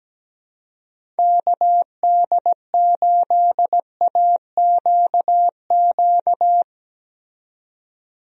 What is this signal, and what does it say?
Morse code, heard as a keyed tone, about 17 wpm, 700 Hz. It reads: KD8AQQ
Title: USA Call Signs - 6 Characters - 17wpm